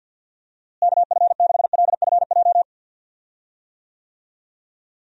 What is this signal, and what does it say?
Morse code, heard as a keyed tone, about 48 wpm, 700 Hz. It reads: KF6LFJ
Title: USA Call Signs - 6 Characters - Repeated 2x - 50wpm